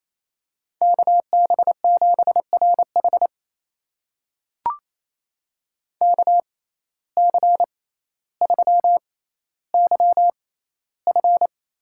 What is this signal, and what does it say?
Morse code, heard as a keyed tone, about 28 wpm, spaced as for 14 wpm, 700 Hz. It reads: KB7RH KC3YF
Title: USA Call Signs - 5 Characters - 6x Character Spacing - 28wpm